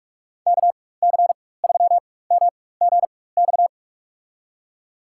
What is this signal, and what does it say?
Morse code, heard as a keyed tone, about 46 wpm, spaced as for 21 wpm, 700 Hz. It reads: KC3MGX
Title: USA Call Signs - 4x Character Spacing - 45wpm